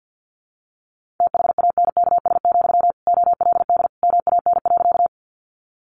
Morse code, heard as a keyed tone, about 50 wpm, 700 Hz. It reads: THANKS, OLD MAN.
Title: Sentences from Top 600 Words 50wpm